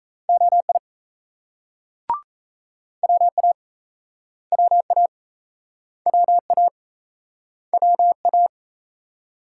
Morse code, heard as a keyed tone, about 39 wpm, 700 Hz. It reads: OI WA WA WA WA